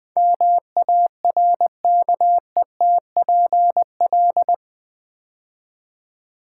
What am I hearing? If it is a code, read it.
Morse code, heard as a keyed tone, about 20 wpm, 700 Hz. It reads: MARKETPL
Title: Build A Word - 4 Letters At A Time - 11-Letter Words - 20wpm